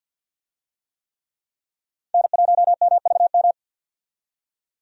Morse code, heard as a keyed tone, about 51 wpm, 700 Hz. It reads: N1MVM
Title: USA Call Signs - 5 Characters - 50wpm